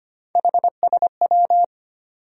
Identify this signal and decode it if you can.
Morse code, heard as a keyed tone, about 25 wpm, 700 Hz. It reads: HSW
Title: USA Call Sign Postfixes - 3 Letters 25wpm